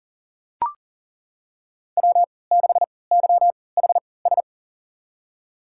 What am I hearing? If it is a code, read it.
Morse code, heard as a keyed tone, about 38 wpm, spaced as for 22 wpm, 700 Hz. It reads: W6YHS
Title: USA Call Signs - 5 Characters - 3x Character Spacing - 40wpm